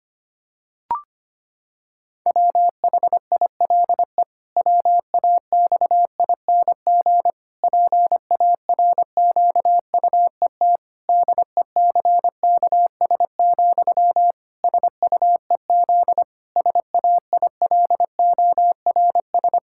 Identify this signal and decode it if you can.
Morse code, heard as a keyed tone, about 25 wpm, 700 Hz. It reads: WHILE WAXING PARQUET DECKS, SUEZ SAILORS
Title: Pangram Challenge v2 - 25wpm